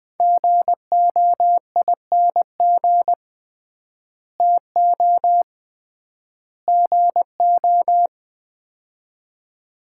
Morse code, heard as a keyed tone, about 21 wpm, 700 Hz. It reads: GOING TO GO
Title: 3x Word-Spacing - Sets of 3 Words - 20wpm